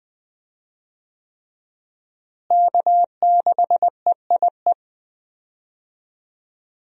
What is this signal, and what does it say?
Morse code, heard as a keyed tone, about 20 wpm, 700 Hz. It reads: K6EIE
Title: USA Call Signs - 5 Characters - Repeated 3x - 20wpm